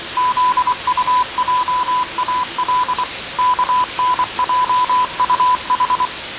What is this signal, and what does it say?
Morse code, heard as a keyed tone, about 24 wpm, 1000 Hz, with noise. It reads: ZUJAL KNJUH
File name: 6824usb